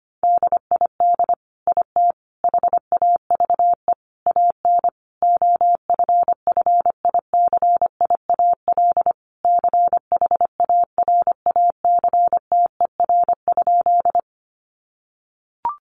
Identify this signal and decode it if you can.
Morse code, heard as a keyed tone, about 25 wpm, 700 Hz. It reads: DID IT HAVE AN OFFICIAL CHARACTER?
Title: Sentences from Top 1200 Words 25wpm